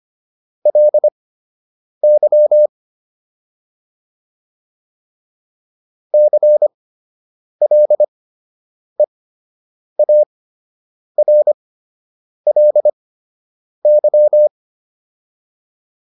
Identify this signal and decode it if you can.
Morse code, heard as a keyed tone, about 25 wpm, spaced as for 8 wpm, 600 Hz. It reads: LY CLEARLY